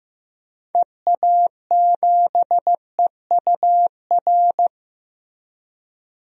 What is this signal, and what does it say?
Morse code, heard as a keyed tone, about 15 wpm, 700 Hz. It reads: EA7EUR
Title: DXCC-100 Call Signs - 6 Characters - Repeated 2x - 15wpm